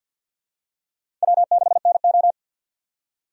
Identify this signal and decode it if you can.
Morse code, heard as a keyed tone, about 51 wpm, 700 Hz. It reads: W6NO
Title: USA Call Signs - 4 Characters - 6x Character Spacing - 50wpm